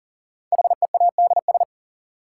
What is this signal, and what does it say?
Morse code, heard as a keyed tone, about 40 wpm, 700 Hz. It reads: HEADS
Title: Sentences from Top 1000 Words - Review Part 3 - New Word Drill - 40wpm